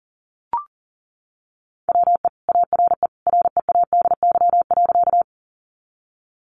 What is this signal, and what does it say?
Morse code, heard as a keyed tone, about 40 wpm, 700 Hz. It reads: WE ARE READY.
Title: Sentences from Top 400 Words 40wpm